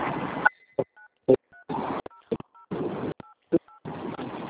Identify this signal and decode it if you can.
DTMF, heard as telephone touch tones, tones of about 120 ms. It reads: 930088
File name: out- - - - - .22656